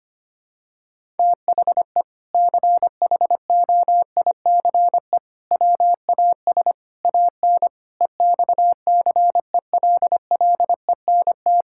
Morse code, heard as a keyed tone, about 25 wpm, 700 Hz. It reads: THE CHOICE WAS AN EXCELLENT
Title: Sentences from Top 1000 Words - Review Part 6 - 25wpm